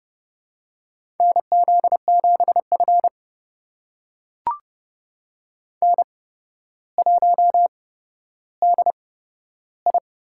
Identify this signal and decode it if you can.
Morse code, heard as a keyed tone, about 30 wpm, spaced as for 13 wpm, 700 Hz. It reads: NZ7F N1DI
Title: USA Call Signs - 4 Characters - 8x Character Spacing - 30wpm